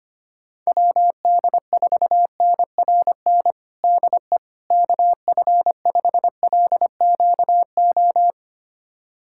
Morse code, heard as a keyed tone, about 25 wpm, 700 Hz. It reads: WD4NRN DE KF5LQO
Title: USA Callsign DE Callsign - 6 Characters - 25wpm